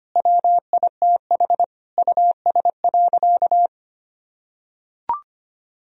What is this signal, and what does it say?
Morse code, heard as a keyed tone, about 25 wpm, 700 Hz. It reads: WITH US.
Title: Sentences from Top 1000 Words - Review Part 3 - 25wpm